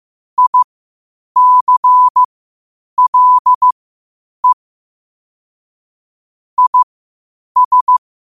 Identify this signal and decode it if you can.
Morse code, heard as a keyed tone, about 15 wpm, spaced as for 8 wpm, 1000 Hz. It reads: ICLE IS
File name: qotdfiles / 8wpm